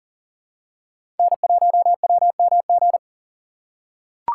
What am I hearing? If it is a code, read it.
Morse code, heard as a keyed tone, about 40 wpm, 700 Hz. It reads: N1WMG
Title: USA Call Signs - 3x Character Spacing - 40wpm